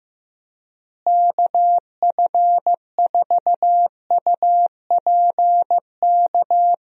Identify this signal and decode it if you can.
Morse code, heard as a keyed tone, about 15 wpm, 700 Hz. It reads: KF4UPK
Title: USA Call Signs - 6 Characters - Repeated 3x - 15wpm